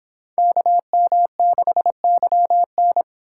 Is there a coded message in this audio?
Yes: Morse code from 0.4 to 3.0 s, heard as a keyed tone, about 25 wpm, 700 Hz. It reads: KM6YN